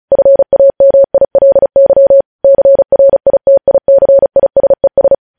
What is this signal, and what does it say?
Morse code, heard as a keyed tone, about 36 wpm, 550 Hz. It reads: FAMILY CRITICISES
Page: News Headlines in Morse Code at 35 WPM